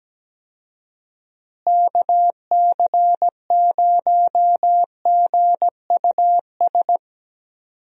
Morse code, heard as a keyed tone, about 17 wpm, 700 Hz. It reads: KC0GUS
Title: USA Call Signs - 6 Characters - Repeated 2x - 17wpm